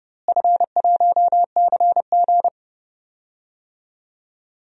Morse code, heard as a keyed tone, about 30 wpm, 700 Hz. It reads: F1CG
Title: DXCC-100 Call Signs - 4 Characters - Repeated 2x - 30wpm